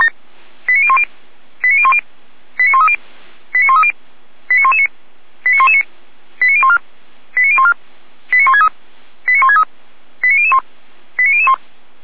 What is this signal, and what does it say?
Five-tone selective call (ZVEI), heard as quick runs of five tones, tones of about 70 ms. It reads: 89010 89010 89130 89130 89119 89119 89014 89014 89163 89163 89001 89001